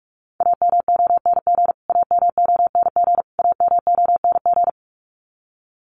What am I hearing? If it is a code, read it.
Morse code, heard as a keyed tone, about 44 wpm, 700 Hz. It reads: AMONG AMONG AMONG
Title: Top 500 Words Repeated 45wpm